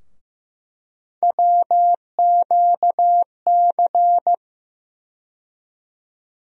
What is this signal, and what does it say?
Morse code, heard as a keyed tone, about 15 wpm, 700 Hz. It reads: WQC
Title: USA Call Sign Postfixes - 3 Letters 15wpm